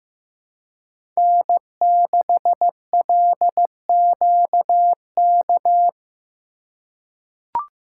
Morse code, heard as a keyed tone, about 15 wpm, 700 Hz. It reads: N6LQK
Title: USA Call Signs - 5 Characters - 3x Character Spacing - 15wpm